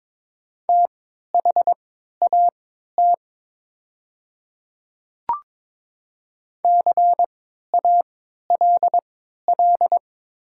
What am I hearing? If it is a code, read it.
Morse code, heard as a keyed tone, about 22 wpm, spaced as for 13 wpm, 700 Hz. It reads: THAT CALL
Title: 3x Character Spacing - Top 100 Words - 22wpm